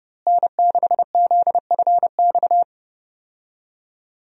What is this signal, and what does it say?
Morse code, heard as a keyed tone, about 30 wpm, 700 Hz. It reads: N6ZFX
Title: USA Call Signs - 5 Characters - Repeated 3x - 30wpm